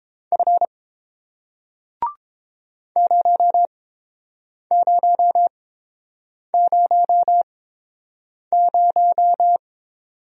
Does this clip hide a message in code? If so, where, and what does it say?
Yes: Morse code from 0.3 to 9.6 s, heard as a keyed tone, about 30 wpm, spaced as for 7 wpm, 700 Hz. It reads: F 0000